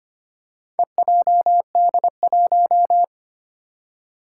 Morse code, heard as a keyed tone, about 25 wpm, 700 Hz. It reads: EJD1
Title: Four Challenging Letter-Number Combinations 25wpm